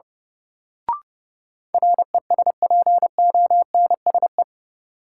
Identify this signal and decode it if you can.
Morse code, heard as a keyed tone, about 30 wpm, 700 Hz. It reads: RESPONSE
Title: Sentences from Top 1400 Words - New Word Drill - 30wpm